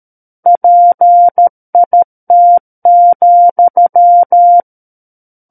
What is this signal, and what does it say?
Morse code, heard as a keyed tone, about 13 wpm, 700 Hz. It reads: PIT,
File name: Book One / 04-Chapter 4 - 13 WPM